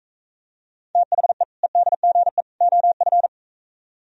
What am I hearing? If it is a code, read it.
Morse code, heard as a keyed tone, about 45 wpm, 700 Hz. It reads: THE EDGE OF